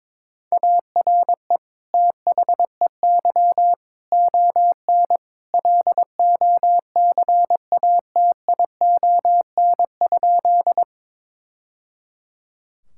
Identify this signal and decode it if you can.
Morse code, heard as a keyed tone, about 22 wpm, 700 Hz. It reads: ARE THEY ON LOCATION?
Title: Sentences from Top 1000 Words - Review Part 6 - 22wpm